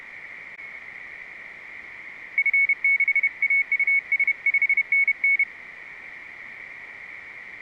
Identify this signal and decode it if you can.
Morse code, heard as a keyed tone, about 32 wpm, 2200 Hz. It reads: F6AUS 5NN